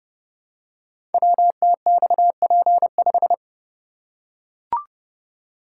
Morse code, heard as a keyed tone, about 30 wpm, 700 Hz. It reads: WTXP5
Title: Code Groups - 5 Letter:Numbers - 30wpm